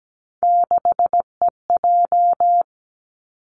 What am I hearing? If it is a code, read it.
Morse code, heard as a keyed tone, about 17 wpm, 700 Hz. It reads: 6EJ
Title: Three Challenging Letter-Number Combinations 17wpm